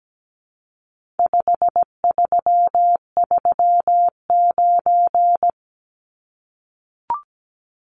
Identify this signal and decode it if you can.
Morse code, heard as a keyed tone, about 17 wpm, 700 Hz. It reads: 5339
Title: Numbers - 4-Digit 17wpm